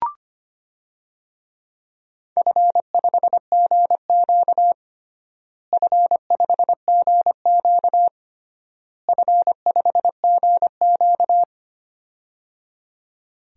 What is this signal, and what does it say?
Morse code, heard as a keyed tone, about 25 wpm, 700 Hz. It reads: F5GQ F5GQ F5GQ
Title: DXCC-100 Call Signs - 4 Characters - Repeated 3x - 25wpm